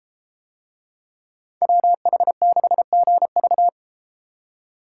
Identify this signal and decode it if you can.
Morse code, heard as a keyed tone, about 32 wpm, 700 Hz. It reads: WH6GV